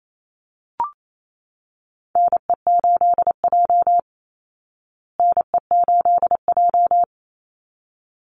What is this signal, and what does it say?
Morse code, heard as a keyed tone, about 28 wpm, 690 Hz. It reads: NE8J NE8J
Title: Call Signs Repeated 3x - 28wpm